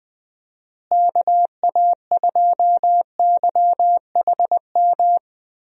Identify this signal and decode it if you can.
Morse code, heard as a keyed tone, about 20 wpm, 700 Hz. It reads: KA2YHM
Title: USA Call Signs - 6 Characters - Repeated 3x - 20wpm